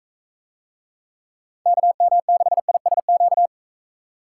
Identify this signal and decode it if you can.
Morse code, heard as a keyed tone, about 44 wpm, 700 Hz. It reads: KM6ISQ